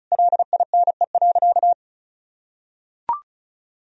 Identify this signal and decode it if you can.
Morse code, heard as a keyed tone, about 36 wpm, 700 Hz. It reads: LINE.
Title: Sentences from Top 1000 Words - Review Part 4 - 35wpm